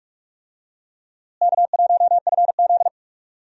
Morse code, heard as a keyed tone, about 44 wpm, 700 Hz. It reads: K1FZ
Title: USA Call Signs - 4 Characters - Repeated 3x - 45wpm